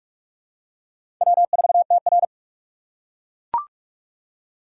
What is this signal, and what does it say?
Morse code, heard as a keyed tone, about 43 wpm, 700 Hz. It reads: W4TR
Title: USA Call Signs - 4 Characters - 4x Character Spacing - 45wpm